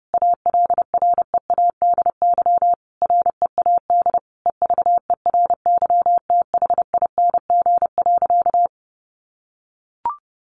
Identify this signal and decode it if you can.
Morse code, heard as a keyed tone, about 30 wpm, 700 Hz. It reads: ALREADY READ EVERYTHING.
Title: Sentences from Top 700 Words 30wpm